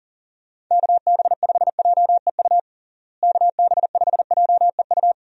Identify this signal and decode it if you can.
Morse code, heard as a keyed tone, about 37 wpm, 700 Hz. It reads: KB5JEU KB5JEU
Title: USA Call Signs - 6 Characters - Repeated 2x - 40wpm